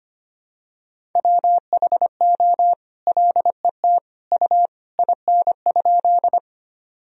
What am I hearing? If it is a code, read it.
Morse code, heard as a keyed tone, about 25 wpm, 700 Hz. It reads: WHO LET U IN?
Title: Sentences from Top 300 Words and Abbreviations 25wpm